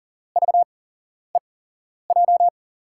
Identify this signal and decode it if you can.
Morse code, heard as a keyed tone, about 39 wpm, 700 Hz. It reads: V E J
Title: USA Call Signs - 6 Characters - 8x Character Spacing - 40wpm